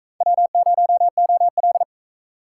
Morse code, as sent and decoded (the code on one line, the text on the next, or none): .-- ----- --- .-..
W0OL